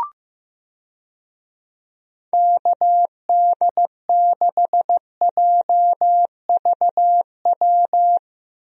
Morse code, heard as a keyed tone, about 15 wpm, 700 Hz. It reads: KD6JVW
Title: USA Call Signs - 6 Characters - 15wpm